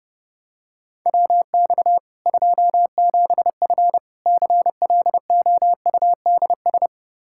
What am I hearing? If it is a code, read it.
Morse code, heard as a keyed tone, about 30 wpm, 700 Hz. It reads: WX 27F CLOUDS
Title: K1USN SST QSO Practice 30wpm